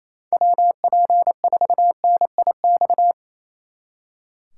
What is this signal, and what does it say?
Morse code, heard as a keyed tone, about 37 wpm, 700 Hz. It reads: WP4NIX